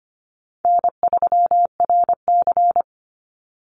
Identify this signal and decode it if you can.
Morse code, heard as a keyed tone, about 25 wpm, 700 Hz. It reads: N3RC
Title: Call Signs Repeated 3x - 25wpm